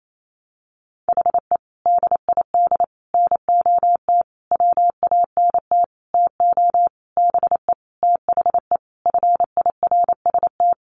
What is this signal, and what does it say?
Morse code, heard as a keyed tone, about 28 wpm, 700 Hz. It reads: HE DID NOT WANT TO BE THE FIRST
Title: Sentences from Top 1300 Words 28wpm